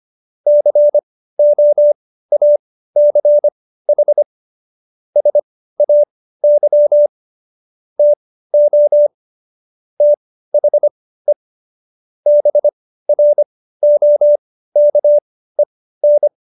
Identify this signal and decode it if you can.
Morse code, heard as a keyed tone, about 25 wpm, spaced as for 15 wpm, 590 Hz. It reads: COACH SAY TO THE BROKEN
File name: Jokes-25-char-15-eff